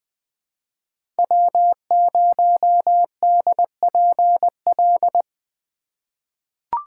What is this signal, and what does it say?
Morse code, heard as a keyed tone, about 20 wpm, 700 Hz. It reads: W0DPL
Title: USA Call Signs - 5 Characters - 2x Character Spacing - 20wpm